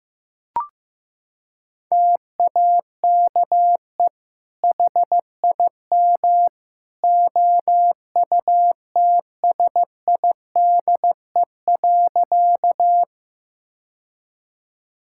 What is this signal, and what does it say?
Morse code, heard as a keyed tone, about 15 wpm, 700 Hz. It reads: TAKE HIM OUTSIDE.
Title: Sentences from Top 1000 Words - Review Part 6 - 15wpm